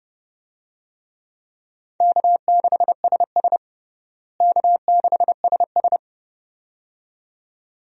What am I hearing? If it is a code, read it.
Morse code, heard as a keyed tone, about 30 wpm, 700 Hz. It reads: K6SS K6SS
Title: USA Call Signs - 4 Characters - Repeated 2x - 30wpm